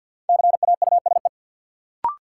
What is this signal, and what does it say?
Morse code, heard as a keyed tone, about 47 wpm, 700 Hz. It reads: CAUSE
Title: Sentences from Top 200 Words - New Word Drill - 50wpm